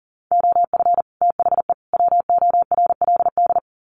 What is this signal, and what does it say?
Morse code, heard as a keyed tone, about 40 wpm, 700 Hz. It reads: OF THE WORLD
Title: Sets of 3 Words 40wpm